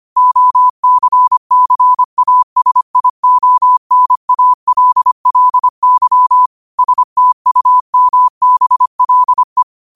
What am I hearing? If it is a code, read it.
Morse code, heard as a keyed tone, about 25 wpm, 1000 Hz. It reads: OCCASIONALLY STUMBLE
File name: qotdfiles / 25wpm